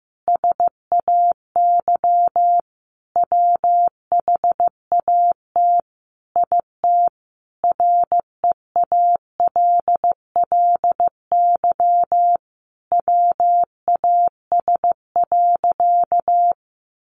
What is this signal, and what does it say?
Morse code, heard as a keyed tone, about 15 wpm, 700 Hz. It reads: SAY WHAT IT REALLY WAS.
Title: Sentences from Top 600 Words 15wpm